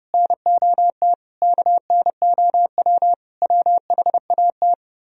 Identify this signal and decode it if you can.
Morse code, heard as a keyed tone, about 37 wpm, 700 Hz. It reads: NOT KNOW WHAT
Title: Speed-Racing - Sets of 5 Words 30wpm